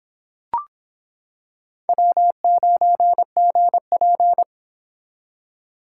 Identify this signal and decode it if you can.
Morse code, heard as a keyed tone, about 26 wpm, 700 Hz. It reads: W9GP